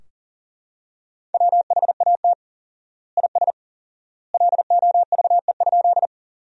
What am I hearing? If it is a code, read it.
Morse code, heard as a keyed tone, about 40 wpm, 700 Hz. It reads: WHAT IS LOVE?
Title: 4x Word-Spacing - Sentences from Top 400 Words - 40wpm